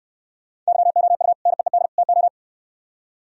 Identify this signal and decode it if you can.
Morse code, heard as a keyed tone, about 62 wpm, 700 Hz. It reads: YOU NEED TO